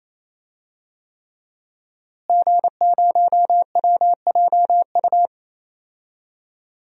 Morse code, heard as a keyed tone, about 28 wpm, 700 Hz. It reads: G0WJU
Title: DXCC-100 Call Signs - 5 Characters - Repeated 3x - 28wpm